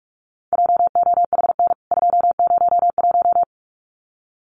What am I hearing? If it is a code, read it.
Morse code, heard as a keyed tone, about 44 wpm, 700 Hz. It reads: JOHN 201
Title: CWT Exchange 45wpm